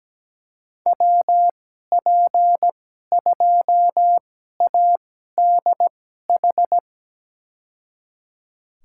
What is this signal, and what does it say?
Morse code, heard as a keyed tone, about 17 wpm, spaced as for 12 wpm, 700 Hz. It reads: WP2ADH
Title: USA Call Signs - 6 Characters - 2x Character Spacing - 17wpm